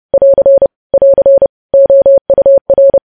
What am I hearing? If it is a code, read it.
Morse code, heard as a keyed tone, about 30 wpm, 550 Hz. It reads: <AR> <AR> OUR
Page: news headlines in morse code at 30 wpm